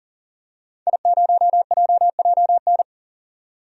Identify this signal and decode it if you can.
Morse code, heard as a keyed tone, about 40 wpm, 700 Hz. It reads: I0JJN
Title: DXCC-100 Call Signs - 5 Characters - 40wpm